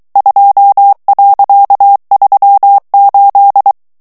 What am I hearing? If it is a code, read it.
Morse code, heard as a keyed tone, about 23 wpm, 800 Hz. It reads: 2.38
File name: HOLLANDAIS texte clair 1 vitesse 22 mots minute